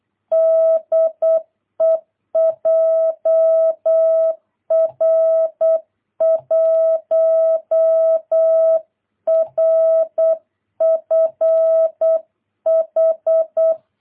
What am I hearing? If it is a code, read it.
Morse code, heard as a keyed tone, about 8 wpm, 650 Hz. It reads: DE JR1RFH
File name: jh8cbh / 2022cw